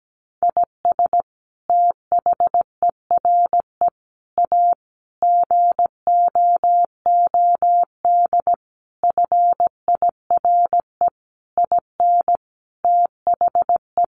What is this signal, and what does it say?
Morse code, heard as a keyed tone, about 17 wpm, 700 Hz. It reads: IS THERE A GOOD FIRE IN THE
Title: Sentences from Top 1400 Words 17wpm